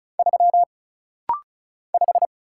Morse code, heard as a keyed tone, about 35 wpm, 700 Hz. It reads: 3 5
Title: Single Letter-Number - Rapid-Fire 35wpm